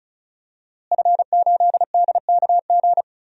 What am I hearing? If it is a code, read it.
Morse code, heard as a keyed tone, about 34 wpm, 700 Hz. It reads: F8DKG
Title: DXCC-100 Call Signs - 5 Characters - Repeated 2x - 35wpm